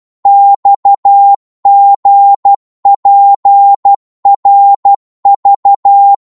Morse code, heard as a keyed tone, about 12 wpm, 800 Hz. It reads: XGPRV
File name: Morse / ltr-12wpm